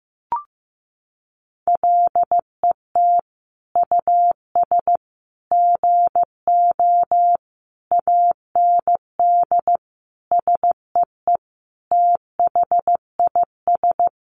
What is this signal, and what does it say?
Morse code, heard as a keyed tone, about 15 wpm, 700 Hz. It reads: LET US GO AND SEE THIS
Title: Sentences from Top 1100 Words 15wpm